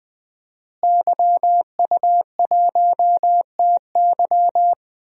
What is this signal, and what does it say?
Morse code, heard as a keyed tone, about 20 wpm, 700 Hz. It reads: YU1TY
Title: DXCC-100 Call Signs - 5 Characters - Repeated 2x - 20wpm